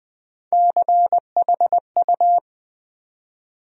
Morse code, heard as a keyed tone, about 20 wpm, 700 Hz. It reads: CHU
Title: USA Call Sign Postfixes - 3 Letters 20wpm